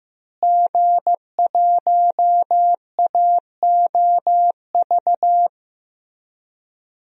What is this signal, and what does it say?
Morse code, heard as a keyed tone, about 15 wpm, 700 Hz. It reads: G1AOV
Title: DXCC-100 Call Signs - 5 Characters - Repeated 3x - 15wpm